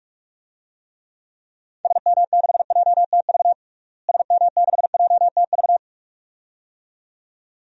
Morse code, heard as a keyed tone, about 41 wpm, 700 Hz. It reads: SM6JTV SM6JTV
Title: DXCC-100 Call Signs - 6 Characters - Repeated 2x - 45wpm